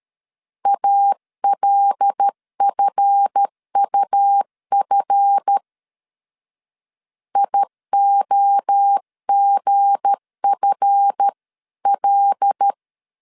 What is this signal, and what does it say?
Morse code, heard as a keyed tone, about 13 wpm, 800 Hz. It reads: ALFUF IOGFL
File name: LEZIONI / 7 GF9Z / 7 GF 12wpm 2122